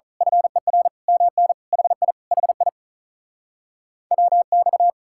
Morse code, heard as a keyed tone, about 32 wpm, 700 Hz. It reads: FER MN HI HI WX
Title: Speed-Racing Common QSO Phrases Encore - 35wpm